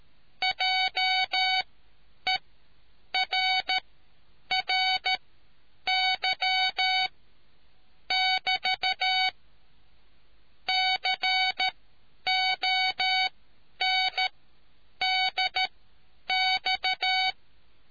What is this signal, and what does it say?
Morse code, heard as a keyed tone, about 13 wpm, spaced as for 8 wpm, 750 Hz. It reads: JERRY <BT> CONDX